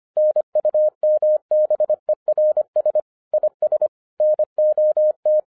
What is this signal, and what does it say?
Morse code, heard as a keyed tone, about 25 wpm, 610 Hz. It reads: NUMBERS IS NOT
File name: FIRST FISTS CONTACT-25wpm